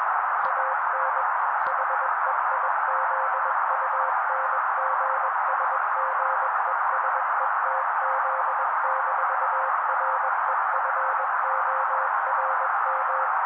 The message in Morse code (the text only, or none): AN HEIZUNGSGESETZ-REFORM